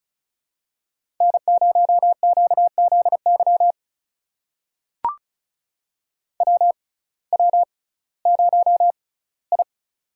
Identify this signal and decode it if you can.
Morse code, heard as a keyed tone, about 36 wpm, spaced as for 19 wpm, 700 Hz. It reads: N0QZY WW0I
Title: USA Call Signs - 6x Character Spacing - 35wpm